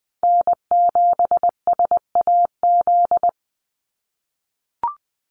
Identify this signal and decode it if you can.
Morse code, heard as a keyed tone, about 20 wpm, 700 Hz. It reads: N7SAZ
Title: USA Call Signs 20wpm